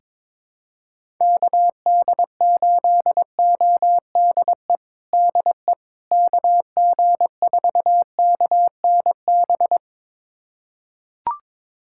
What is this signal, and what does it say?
Morse code, heard as a keyed tone, about 22 wpm, 700 Hz. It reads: KD8ODE DE KG4KNB
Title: USA Callsign DE Callsign - 6 Characters - 22wpm